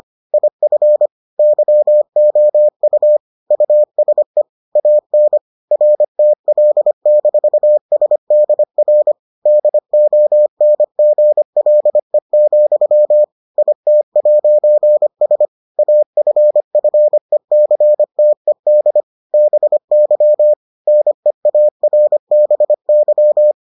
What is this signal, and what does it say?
Morse code, heard as a keyed tone, about 25 wpm, 600 Hz. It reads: IF YOU USE AN RTL-SDR DONGLE, IT'S AFFECTED BY NEARBY